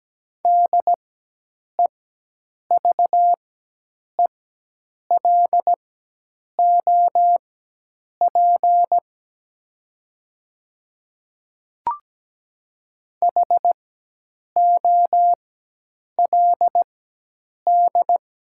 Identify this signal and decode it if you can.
Morse code, heard as a keyed tone, about 17 wpm, spaced as for 8 wpm, 700 Hz. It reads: DEVELOP HOLD
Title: 4x Character Spacing - Top 500 Words - 17wpm